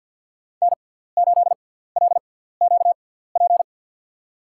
Morse code, heard as a keyed tone, about 47 wpm, spaced as for 17 wpm, 700 Hz. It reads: N8LQP
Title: USA Call Signs - 5 Characters - 6x Character Spacing - 50wpm